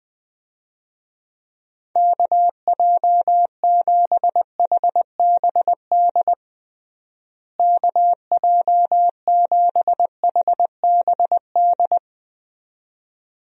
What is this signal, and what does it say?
Morse code, heard as a keyed tone, about 20 wpm, 700 Hz. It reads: KJ7HBD KJ7HBD
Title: USA Call Signs - 6 Characters - Repeated 2x - 20wpm